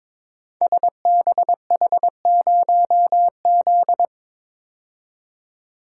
Morse code, heard as a keyed tone, about 22 wpm, 700 Hz. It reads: SBH0Z
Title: Code Groups - 5 Letter:Numbers - 22wpm